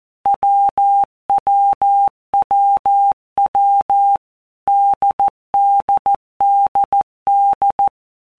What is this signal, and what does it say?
Morse code, heard as a keyed tone, about 14 wpm, 800 Hz. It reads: WWWW DDDD
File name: Basic Letter Groups By Blocks / Groups@13 / AWUNDG@13